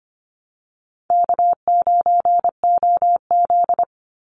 Morse code, heard as a keyed tone, about 25 wpm, 700 Hz. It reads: K9OZ
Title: Easy Call Signs - 25wpm